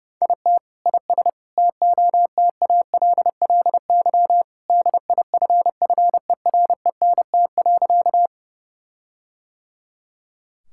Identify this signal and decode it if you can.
Morse code, heard as a keyed tone, about 32 wpm, 700 Hz. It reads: IT IS TOTALLY DIFFERENT.
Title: Sentences from Top 1600 Words 30wpm